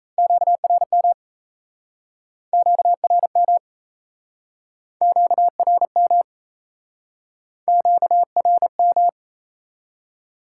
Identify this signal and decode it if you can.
Morse code, heard as a keyed tone, about 35 wpm, 700 Hz. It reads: QRM QRM QRM QRM